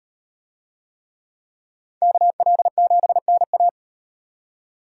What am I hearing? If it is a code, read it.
Morse code, heard as a keyed tone, about 38 wpm, 700 Hz. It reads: KL7NA